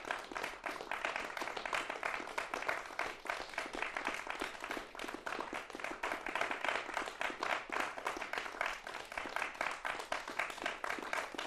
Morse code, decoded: none

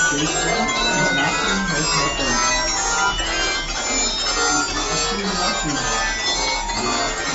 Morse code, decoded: none

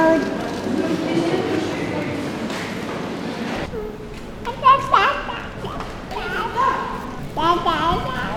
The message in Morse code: none

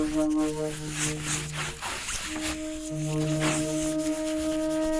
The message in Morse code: none

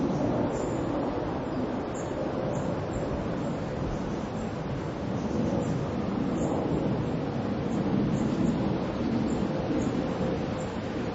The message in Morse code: none

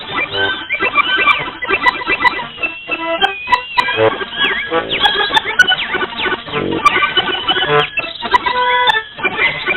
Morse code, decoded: none

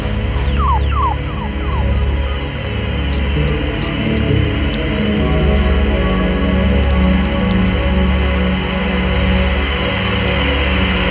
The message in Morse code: none

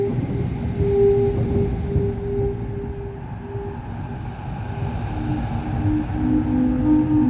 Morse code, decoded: none

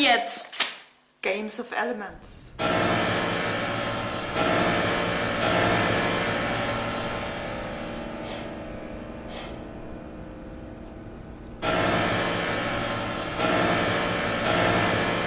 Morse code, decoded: none

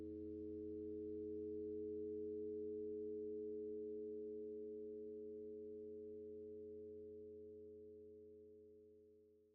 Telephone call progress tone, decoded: dial tone

